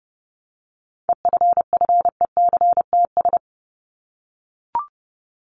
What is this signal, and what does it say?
Morse code, heard as a keyed tone, about 30 wpm, 700 Hz. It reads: EFFECTS
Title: ES:S Suffix - Top 100 Words with Base Words 30wpm